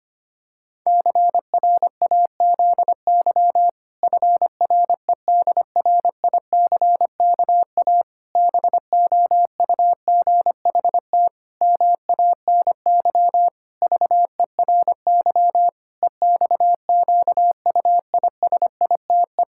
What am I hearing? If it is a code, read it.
Morse code, heard as a keyed tone, about 25 wpm, 700 Hz. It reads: CRAZY FREDRICKA BOUGHT MANY VERY EXQUISITE
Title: Pangram Challenge v2 - 25wpm